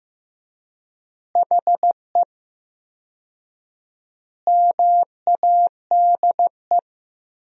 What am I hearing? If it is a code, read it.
Morse code, heard as a keyed tone, about 16 wpm, 700 Hz. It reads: HE MADE
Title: 4x Word-Spacing - Sentences from Top 300 Words - 15wpm